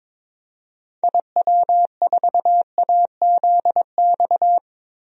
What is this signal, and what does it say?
Morse code, heard as a keyed tone, about 22 wpm, 700 Hz. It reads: IW4AZX
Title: DXCC-100 Call Signs - 6 Characters - 22wpm